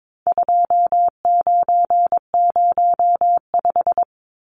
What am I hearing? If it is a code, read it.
Morse code, heard as a keyed tone, about 22 wpm, 700 Hz. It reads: 2905
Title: Numbers - 4-Digit 22wpm